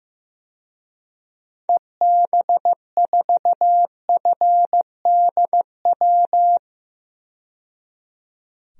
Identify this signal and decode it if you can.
Morse code, heard as a keyed tone, about 15 wpm, 700 Hz. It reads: EB4FDW
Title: DXCC-100 Call Signs - 6 Characters - 15wpm